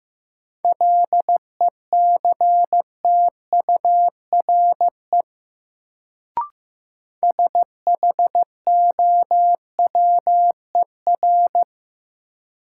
Morse code, heard as a keyed tone, about 15 wpm, 700 Hz. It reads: LECTURE SHOWER
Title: Sentences from Top 1600 Words - New Word Drill Alt - 15wpm